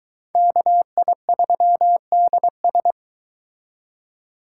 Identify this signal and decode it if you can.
Morse code, heard as a keyed tone, about 23 wpm, 700 Hz. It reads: KI3DS